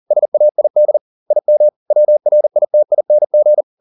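Morse code, heard as a keyed tone, about 40 wpm, 600 Hz. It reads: SAID IM WRITING
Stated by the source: Morse code at 40 wpm, practice text, CW med Gnister.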